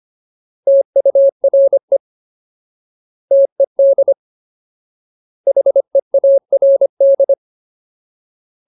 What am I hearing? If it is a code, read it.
Morse code, heard as a keyed tone, about 25 wpm, 550 Hz. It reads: TURE TED HEARD
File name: Ultimate Word List / 25WPM Ultimate